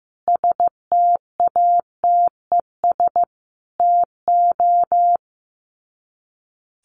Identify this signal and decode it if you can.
Morse code, heard as a keyed tone, about 15 wpm, 700 Hz. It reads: STATES TO
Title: Sets of 5 Words Encore 15wpm